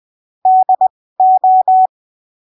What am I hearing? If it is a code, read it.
Morse code, heard as a keyed tone, about 20 wpm, spaced as for 15 wpm, 750 Hz. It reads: DO
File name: arquivos / texto / texto 23 20ppm